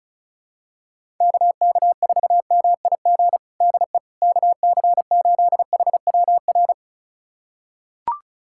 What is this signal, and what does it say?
Morse code, heard as a keyed tone, about 36 wpm, 700 Hz. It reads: KK4MIG DE KC8HWR
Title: USA Callsign DE Callsign 35wpm